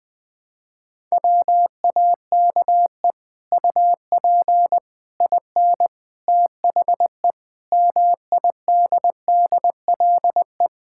Morse code, heard as a keyed tone, about 20 wpm, 700 Hz. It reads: WAKE UP IN THE MIDDLE
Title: Sets of 8 Words - 20wpm